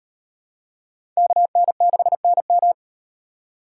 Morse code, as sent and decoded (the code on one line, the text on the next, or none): -.- -. -.... -. --
KN6NM